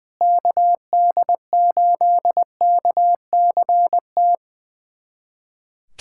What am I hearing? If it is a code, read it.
Morse code, heard as a keyed tone, about 21 wpm, 700 Hz. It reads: KD8KCT
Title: USA Callsign DE Callsign - 6 Characters - 20wpm